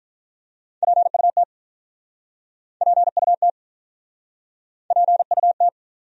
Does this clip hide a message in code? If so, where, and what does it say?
Yes: Morse code from 0.8 to 5.7 s, heard as a keyed tone, about 47 wpm, 700 Hz. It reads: PUT PUT PUT